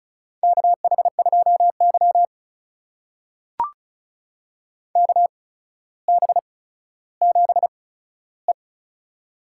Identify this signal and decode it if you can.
Morse code, heard as a keyed tone, about 36 wpm, spaced as for 15 wpm, 700 Hz. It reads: KH2Y KB7E
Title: USA Call Signs - 4 Characters - 8x Character Spacing - 35wpm